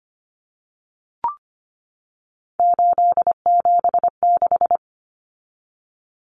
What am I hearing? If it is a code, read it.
Morse code, heard as a keyed tone, about 25 wpm, 700 Hz. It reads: 876
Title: Numbers - 3-Digit 25wpm